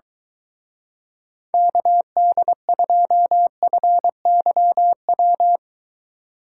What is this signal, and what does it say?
Morse code, heard as a keyed tone, about 23 wpm, 700 Hz. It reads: KD2FYW